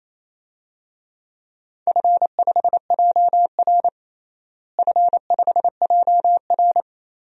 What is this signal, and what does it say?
Morse code, heard as a keyed tone, about 28 wpm, 700 Hz. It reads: F5JR F5JR
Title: DXCC-100 Call Signs - 4 Characters - Repeated 3x - 28wpm